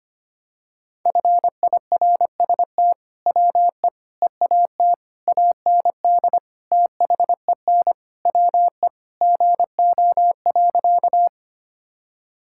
Morse code, heard as a keyed tone, about 25 wpm, 700 Hz. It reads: FIRST WE EAT AND THEN WE GO.
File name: practice / 25wpm / Sentences with 2-Word Phrases - 25wpm